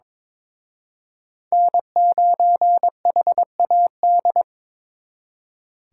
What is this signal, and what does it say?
Morse code, heard as a keyed tone, about 26 wpm, 700 Hz. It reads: N9HAD
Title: Speed Racing - USA 1x3 Call Signs 22wpm